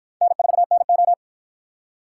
Morse code, heard as a keyed tone, about 46 wpm, 700 Hz. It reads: N4NO